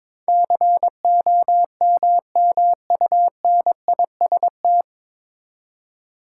Morse code, heard as a keyed tone, about 21 wpm, 700 Hz. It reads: COMMUNIST
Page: Morse Code Ninja practice 22wpm